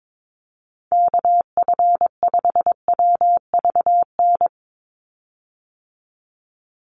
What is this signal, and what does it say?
Morse code, heard as a keyed tone, about 22 wpm, 700 Hz. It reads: KF5WVN
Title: USA Call Signs 22wpm